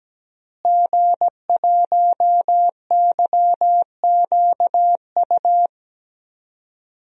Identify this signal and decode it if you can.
Morse code, heard as a keyed tone, about 17 wpm, 700 Hz. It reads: G1YQU
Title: DXCC-100 Call Signs - 5 Characters - Repeated 2x - 17wpm